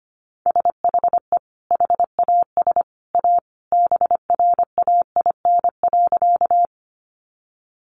Morse code, heard as a keyed tone, about 25 wpm, 700 Hz. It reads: SHE HAS A BRAIN.
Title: Sentences from Top 1100 Words 25wpm